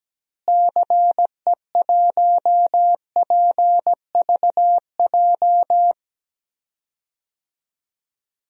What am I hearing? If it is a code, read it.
Morse code, heard as a keyed tone, about 17 wpm, 700 Hz. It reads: CE1PVJ
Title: DXCC-100 Call Signs - 6 Characters - Repeated 3x - 17wpm